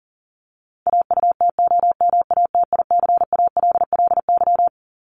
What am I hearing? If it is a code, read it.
Morse code, heard as a keyed tone, about 40 wpm, 700 Hz. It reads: AUTOMATICALLY
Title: Top 1751-2000 Words 40wpm